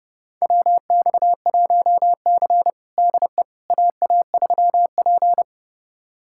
Morse code, heard as a keyed tone, about 32 wpm, 700 Hz. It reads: WX1C DE AA3P
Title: USA Callsign DE Callsign - 4 Characters - 30wpm